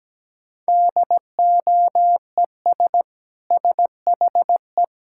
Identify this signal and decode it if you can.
Morse code, heard as a keyed tone, about 17 wpm, 700 Hz. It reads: DOES SHE
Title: Sentences from Top 1000 Words - Review Part 6 - 17wpm